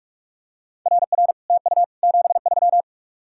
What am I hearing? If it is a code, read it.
Morse code, heard as a keyed tone, about 46 wpm, 700 Hz. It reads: RR TU 73